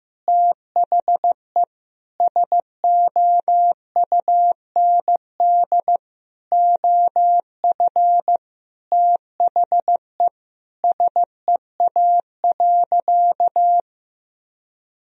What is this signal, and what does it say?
Morse code, heard as a keyed tone, about 15 wpm, 700 Hz. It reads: THE SOUND OF THE SEA.
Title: Sentences from Top 1000 Words - Review Part 1 - 15wpm